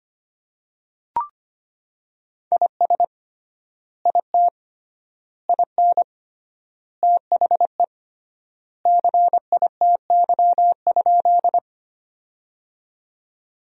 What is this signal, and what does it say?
Morse code, heard as a keyed tone, about 25 wpm, 700 Hz. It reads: IS IT IN THE CITY?
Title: 3x Word-Spacing - Sentences from Top 300 Words - 25wpm